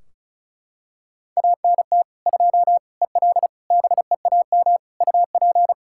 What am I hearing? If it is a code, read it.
Morse code, heard as a keyed tone, about 36 wpm, 700 Hz. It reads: ANT 2 EL BEAM UP
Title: Practice QSO - Exchange 2 - Info - 35wpm